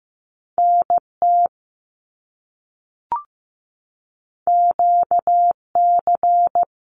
Morse code, heard as a keyed tone, about 15 wpm, 700 Hz. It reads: NT QC
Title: Canadian Provinces and Territories - Repeated 3x - 15wpm